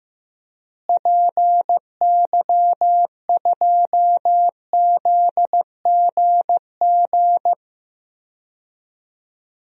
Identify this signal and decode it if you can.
Morse code, heard as a keyed tone, about 16 wpm, 700 Hz. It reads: PY2ZGG
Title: DXCC-100 Call Signs - 6 Characters - Repeated 2x - 15wpm